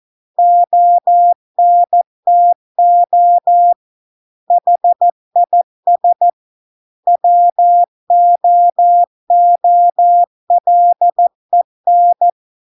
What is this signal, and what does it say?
Morse code, heard as a keyed tone, about 14 wpm, 700 Hz. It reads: ONTO HIS WOOLEN